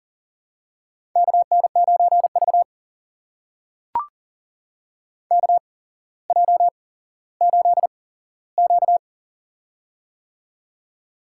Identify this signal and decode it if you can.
Morse code, heard as a keyed tone, about 40 wpm, spaced as for 17 wpm, 700 Hz. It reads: KN9V KJ8Q
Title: USA Call Signs - 4 Characters - 8x Character Spacing - 40wpm